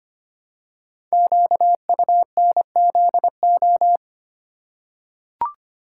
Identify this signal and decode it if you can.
Morse code, heard as a keyed tone, about 25 wpm, 700 Hz. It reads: QUNZO
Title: Code Groups - 5 Letters - 25wpm